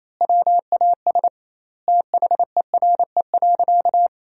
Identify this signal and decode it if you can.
Morse code, heard as a keyed tone, about 28 wpm, 700 Hz. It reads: WAS THERE.
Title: 2x Word-Spacing - Sentences from Top 500 Words - 28wpm